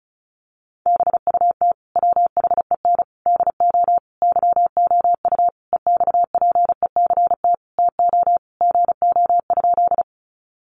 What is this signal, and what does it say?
Morse code, heard as a keyed tone, about 34 wpm, 700 Hz. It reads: BUT WHEN DO YOU EXPECT TO GO?
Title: Sentences from Top 1100 Words 35wpm